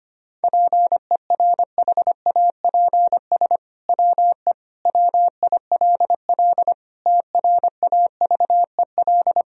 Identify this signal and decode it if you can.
Morse code, heard as a keyed tone, about 25 wpm, 700 Hz. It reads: PERHAPS WE WILL TRAVEL
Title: Sentences from Top 1000 Words - Review Part 2 - 25wpm